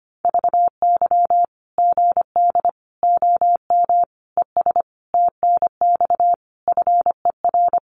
Morse code, heard as a keyed tone, about 25 wpm, 700 Hz. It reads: VY GD OM ES TNX FER
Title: Common QSO Phrases 25wpm